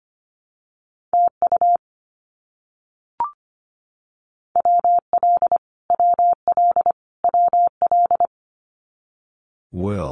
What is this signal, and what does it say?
Morse code, heard as a keyed tone, about 25 wpm, 700 Hz. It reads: TU WL WL WL
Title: Abbreviations Repeated 3x - 25wpm